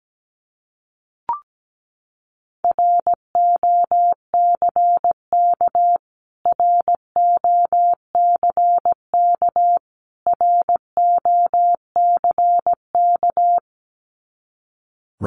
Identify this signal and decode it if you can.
Morse code, heard as a keyed tone, about 17 wpm, 700 Hz. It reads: ROCK ROCK ROCK
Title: Top 500 Words Repeated 17wpm